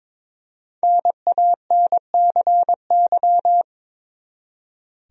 Morse code, heard as a keyed tone, about 22 wpm, 700 Hz. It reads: NANCY
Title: USA Common Names - Max 5 Letters - 22wpm